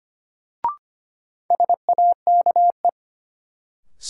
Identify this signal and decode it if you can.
Morse code, heard as a keyed tone, about 25 wpm, 700 Hz. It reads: SAKE
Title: Sets of 4 Words - New Word Drill - 25wpm